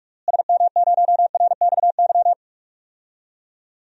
Morse code, heard as a keyed tone, about 46 wpm, 700 Hz. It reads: SM0RXY
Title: DXCC-100 Call Signs - 6 Characters - Repeated 3x - 45wpm